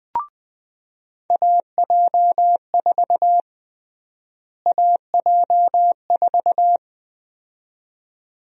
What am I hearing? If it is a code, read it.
Morse code, heard as a keyed tone, about 20 wpm, 700 Hz. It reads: AJ4 AJ4
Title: USA Call Sign Prefixes - Repeated 2x - 20wpm